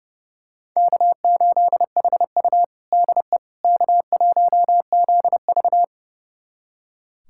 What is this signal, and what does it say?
Morse code, heard as a keyed tone, about 30 wpm, 700 Hz. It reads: K8HU DE K1ZV
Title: USA Callsign DE Callsign - 4 Characters - 30wpm